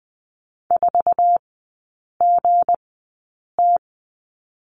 Morse code, heard as a keyed tone, about 20 wpm, spaced as for 8 wpm, 700 Hz. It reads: 4GT
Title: 184 Lesson 30e - Letter b - Three Characters 20wpm